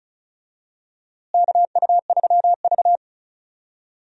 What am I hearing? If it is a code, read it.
Morse code, heard as a keyed tone, about 36 wpm, 700 Hz. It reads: KU3V
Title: USA Call Signs - 4 Characters - Repeated 2x - 35wpm